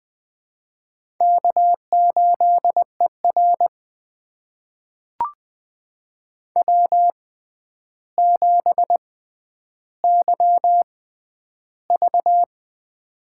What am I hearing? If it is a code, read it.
Morse code, heard as a keyed tone, about 20 wpm, spaced as for 10 wpm, 700 Hz. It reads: K8ER W7YV
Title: USA Call Signs - 4 Characters - 6x Character Spacing - 20wpm